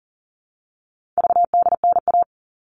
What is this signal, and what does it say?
Morse code, heard as a keyed tone, about 40 wpm, 700 Hz. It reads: VDNA
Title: All Cut Numbers - 4-Digit 40wpm